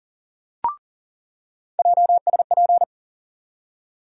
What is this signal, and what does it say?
Morse code, heard as a keyed tone, about 40 wpm, 700 Hz. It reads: JSP
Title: USA Call Sign Postfixes - 3 Letters 40wpm